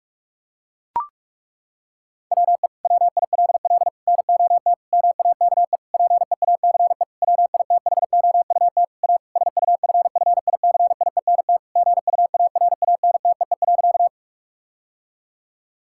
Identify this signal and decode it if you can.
Morse code, heard as a keyed tone, about 43 wpm, 700 Hz. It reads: WE WILL NOT MAKE PEACE WITHOUT A SUFFICIENT GUARANTEE.
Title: Sentences from Top 1600 Words 45wpm